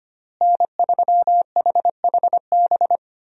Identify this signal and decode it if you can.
Morse code, heard as a keyed tone, about 25 wpm, 700 Hz. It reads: N3HHB
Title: USA Call Signs - 5 Characters - Repeated 2x - 25wpm